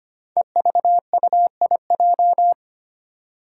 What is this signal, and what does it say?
Morse code, heard as a keyed tone, about 25 wpm, 700 Hz. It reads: EVUIJ
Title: Five Challenging Letter-Number Combinations 25wpm